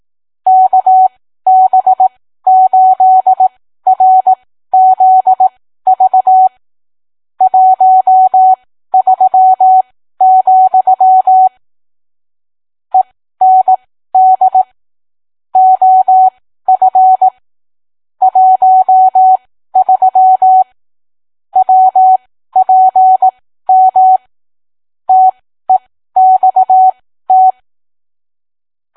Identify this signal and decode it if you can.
Morse code, heard as a keyed tone, about 18 wpm, spaced as for 13 wpm, 750 Hz. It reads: KB8RZV 13, END OF 13 WPM TEXT